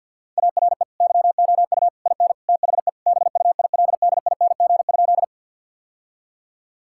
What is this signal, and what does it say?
Morse code, heard as a keyed tone, about 50 wpm, 700 Hz. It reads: ARE YOU IN THE BUILDING?
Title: Sentences from Top 1000 Words - Review Part 5 - 50wpm